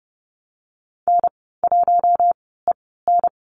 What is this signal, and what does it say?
Morse code, heard as a keyed tone, about 30 wpm, spaced as for 17 wpm, 700 Hz. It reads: N1EN
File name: Lesson 14g - Number 2 - Call Signs 30wpm